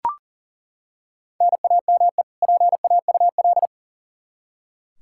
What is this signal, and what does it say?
Morse code, heard as a keyed tone, about 37 wpm, 700 Hz. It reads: NAME PAUL
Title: Practice QSO - Exchange 1 - Beginning - 40wpm